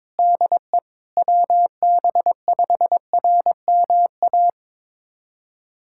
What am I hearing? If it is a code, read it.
Morse code, heard as a keyed tone, about 21 wpm, 700 Hz. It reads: DE WB5RMA
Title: USA Callsign DE Callsign - 6 Characters - 22wpm